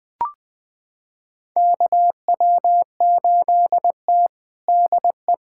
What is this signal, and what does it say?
Morse code, heard as a keyed tone, about 20 wpm, 700 Hz. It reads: KW8T DE
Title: USA Callsign DE Callsign - 4 Characters - 20wpm